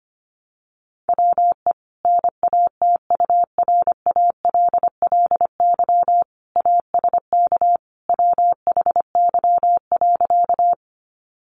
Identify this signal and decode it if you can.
Morse code, heard as a keyed tone, about 25 wpm, 700 Hz. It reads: WE NATURALLY ASK WHY.
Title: Sentences from Top 1300 Words 25wpm